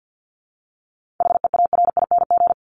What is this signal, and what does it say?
Morse code, heard as a keyed tone, about 50 wpm, 700 Hz. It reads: HEARING
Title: Top 1751-2000 Words 50wpm